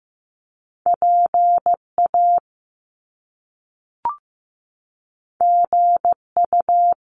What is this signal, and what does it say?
Morse code, heard as a keyed tone, about 15 wpm, 700 Hz. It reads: PA GU
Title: Top Two Letter Combinations in Words 15wpm